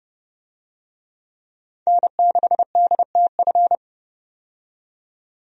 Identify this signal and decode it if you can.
Morse code, heard as a keyed tone, about 30 wpm, 700 Hz. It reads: N6DTF
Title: USA Call Signs - 5 Characters - 30wpm